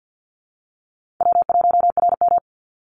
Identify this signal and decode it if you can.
Morse code, heard as a keyed tone, about 50 wpm, 700 Hz. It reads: W1RM
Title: Call Signs - 50wpm